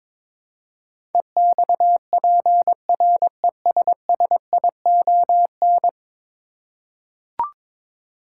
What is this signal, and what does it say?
Morse code, heard as a keyed tone, about 22 wpm, 700 Hz. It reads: EXPRESSION
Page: morse-code-ninja-practice-22wpm